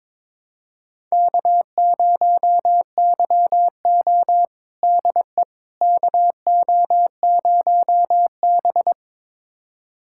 Morse code, heard as a keyed tone, about 22 wpm, 700 Hz. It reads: K0YO DE KO0B
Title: USA Callsign DE Callsign - 4 Characters - 22wpm